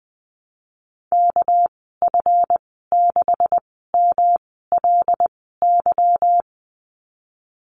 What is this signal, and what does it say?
Morse code, heard as a keyed tone, about 20 wpm, spaced as for 15 wpm, 700 Hz. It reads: KF6MLY
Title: 207 Lesson 33g - Letter j - Call Signs 20wpm